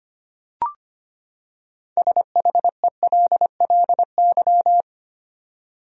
Morse code, heard as a keyed tone, about 25 wpm, 700 Hz. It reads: SHELLY
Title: USA Common Names - 25wpm